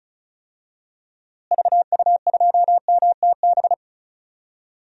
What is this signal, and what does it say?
Morse code, heard as a keyed tone, about 36 wpm, 700 Hz. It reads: VU2MTB
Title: DXCC-100 Call Signs - 6 Characters - Repeated 3x - 35wpm